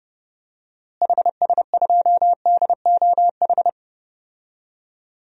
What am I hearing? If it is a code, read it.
Morse code, heard as a keyed tone, about 30 wpm, 700 Hz. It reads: HS2DOH
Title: DXCC-100 Call Signs - 6 Characters - Repeated 2x - 30wpm